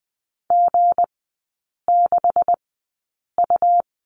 Morse code, heard as a keyed tone, about 20 wpm, spaced as for 8 wpm, 690 Hz. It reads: G6U
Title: 172 Lesson 28e - Number 9 - Three Characters 20wpm